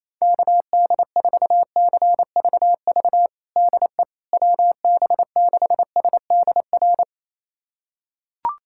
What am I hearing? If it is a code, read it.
Morse code, heard as a keyed tone, about 28 wpm, 700 Hz. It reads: KD4CVV DE WB6SDR
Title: USA Callsign DE Callsign - 6 Characters - 28wpm